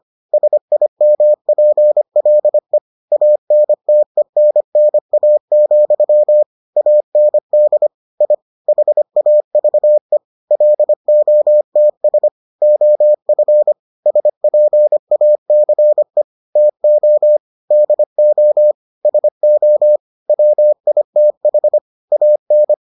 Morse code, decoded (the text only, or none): SIMPLE ANTENNA, AND I HAVE LOTS OF SPACE TO DO SO WITH AN